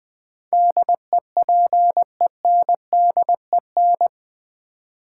Morse code, heard as a keyed tone, about 20 wpm, 700 Hz. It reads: DEPENDEN
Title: Build A Word - 4 Letters At A Time - 10-Letter Words - 20wpm